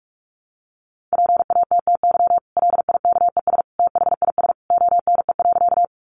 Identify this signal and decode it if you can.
Morse code, heard as a keyed tone, about 44 wpm, 700 Hz. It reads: PATTY LIKES THIS ONE.
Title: Additional Easy Phrases 45wpm